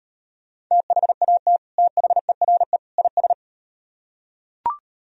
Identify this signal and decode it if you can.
Morse code, heard as a keyed tone, about 38 wpm, 700 Hz. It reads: THAT THERE IS